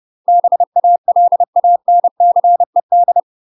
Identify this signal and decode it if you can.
Morse code, heard as a keyed tone, about 30 wpm, 700 Hz. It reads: BALANCED
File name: UK MASTER QSOs TRAINING / 30WPM / UK MASTER QSOs 0045